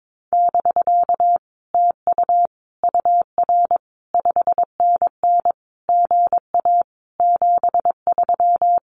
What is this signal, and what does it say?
Morse code, heard as a keyed tone, about 22 wpm, 700 Hz. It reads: <BK> TU UR 5NN GA 73
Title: Common QSO Phrases Encore 22wpm